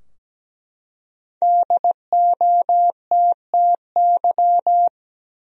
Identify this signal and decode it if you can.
Morse code, heard as a keyed tone, about 17 wpm, 700 Hz. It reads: DOTTY
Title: USA Common Names - Max 5 Letters - 17wpm